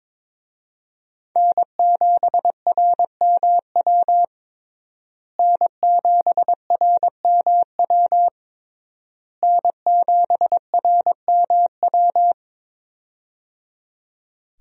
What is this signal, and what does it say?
Morse code, heard as a keyed tone, about 22 wpm, 700 Hz. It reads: N7RMW N7RMW N7RMW
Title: USA Call Signs - 5 Characters - Repeated 3x - 22wpm